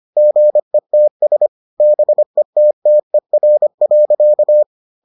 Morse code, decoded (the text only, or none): GETS BETTER.